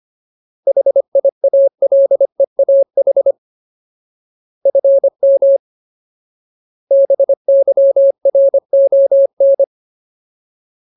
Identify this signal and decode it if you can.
Morse code, heard as a keyed tone, about 25 wpm, 550 Hz. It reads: HIALEAH FM BYRON
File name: Ultimate Word List / 25WPM Ultimate